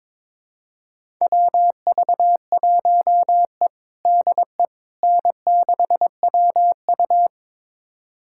Morse code, heard as a keyed tone, about 22 wpm, 700 Hz. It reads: WV1E DE N6WU
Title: USA Callsign DE Callsign - 4 Characters - 22wpm